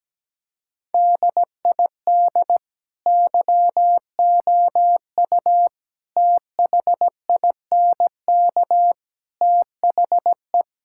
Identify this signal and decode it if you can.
Morse code, heard as a keyed tone, about 17 wpm, 700 Hz. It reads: DID YOU THINK THE
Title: Sentences from Top 1000 Words - Review Part 5 - 17wpm